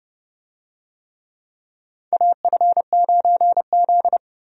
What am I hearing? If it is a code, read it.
Morse code, heard as a keyed tone, about 30 wpm, 700 Hz. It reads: AF9Z
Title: USA Call Signs - 4 Characters - 30wpm